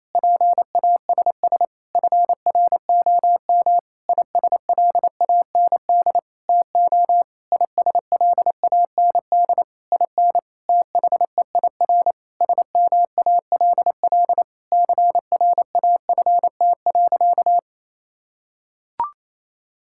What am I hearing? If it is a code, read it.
Morse code, heard as a keyed tone, about 27 wpm, 700 Hz. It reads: PASS FROM ISLAND TO ISLAND IN THEIR SMALL CRAFT.
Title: Sentences from Top 1000 Words - Review Part 6 - 28wpm